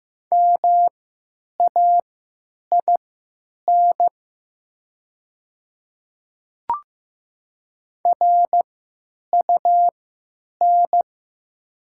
Morse code, heard as a keyed tone, about 15 wpm, spaced as for 9 wpm, 700 Hz. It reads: MAIN RUN
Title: 3x Character Spacing - Top 300 Words - 15wpm